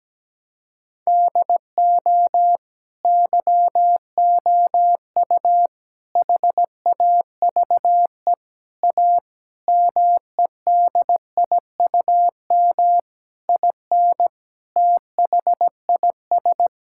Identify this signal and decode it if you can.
Morse code, heard as a keyed tone, about 17 wpm, 700 Hz. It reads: DO YOU HAVE A MEDIUM IN THIS
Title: Sentences from Top 1000 Words - Review Part 6 - 17wpm